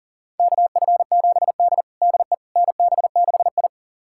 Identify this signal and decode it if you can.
Morse code, heard as a keyed tone, about 40 wpm, 700 Hz. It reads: KF7D DE NB6I
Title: USA Callsign DE Callsign - 4 Characters - 40wpm